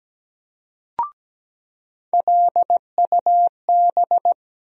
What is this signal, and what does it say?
Morse code, heard as a keyed tone, about 17 wpm, 700 Hz. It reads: LUB
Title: USA Call Sign Postfixes - 3 Letters 17wpm